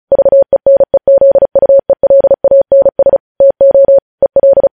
Morse code, heard as a keyed tone, about 36 wpm, 550 Hz. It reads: VENEZUELANS TO EL
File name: podcast / News headlines-1x-35x35